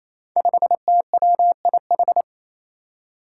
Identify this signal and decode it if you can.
Morse code, heard as a keyed tone, about 28 wpm, 700 Hz. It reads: 5TWIH
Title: Five Challenging Letter-Number Combinations 28wpm